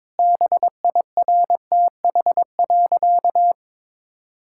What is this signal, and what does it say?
Morse code, heard as a keyed tone, about 22 wpm, 700 Hz. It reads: BIRTH.
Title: Sentences from Top 1000 Words - Review Part 5 - 22wpm